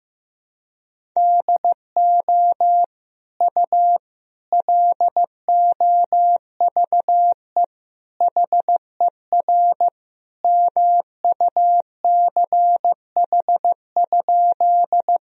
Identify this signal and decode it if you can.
Morse code, heard as a keyed tone, about 16 wpm, 700 Hz. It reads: DO U LOVE HER MUCH?
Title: Sentences from Top 400 Words and Abbreviations 15wpm